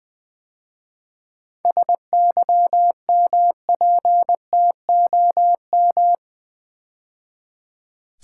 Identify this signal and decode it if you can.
Morse code, heard as a keyed tone, about 20 wpm, 700 Hz. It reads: SYMPTOM
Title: 7 Letter Words - First 3 Letters Given - 20wpm